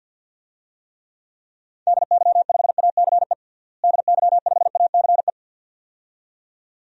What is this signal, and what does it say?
Morse code, heard as a keyed tone, about 50 wpm, 700 Hz. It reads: DY5ACE DY5ACE
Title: DXCC-100 Call Signs - 6 Characters - Repeated 2x - 50wpm